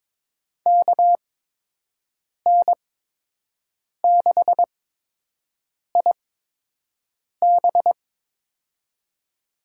Morse code, heard as a keyed tone, about 22 wpm, spaced as for 6 wpm, 700 Hz. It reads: KN6IB
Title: USA Call Signs - 5 Characters - 8x Character Spacing - 22wpm